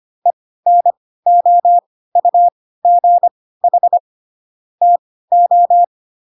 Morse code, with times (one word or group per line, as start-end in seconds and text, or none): 0.3-4.0 ENOUGH
4.8-5.8 TO